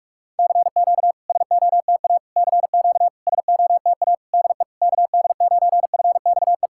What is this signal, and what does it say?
Morse code, heard as a keyed tone, about 44 wpm, 690 Hz. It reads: CQ SOTA CQ SOTA DE KD9FXE
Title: SOTA QSO Practice 45wpm